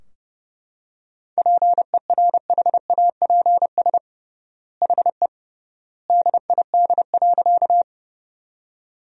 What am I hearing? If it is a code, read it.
Morse code, heard as a keyed tone, about 32 wpm, 700 Hz. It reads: PERHAPS HE DID.
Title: 3x Word-Spacing - Sentences from Top 500 Words - 30wpm